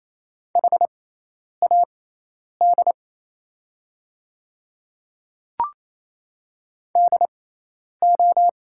Morse code, heard as a keyed tone, about 28 wpm, spaced as for 10 wpm, 700 Hz. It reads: HAD DO
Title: 6x Character Spacing - Top 100 Words - 28wpm